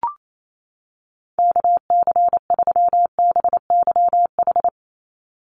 Morse code, heard as a keyed tone, about 28 wpm, 700 Hz. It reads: KC3BYH
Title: USA Call Signs 28wpm